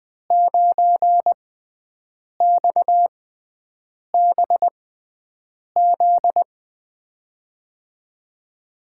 Morse code, heard as a keyed tone, about 20 wpm, spaced as for 7 wpm, 700 Hz. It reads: 9XBZ
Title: USA Call Signs - 5 Characters - 6x Character Spacing - 20wpm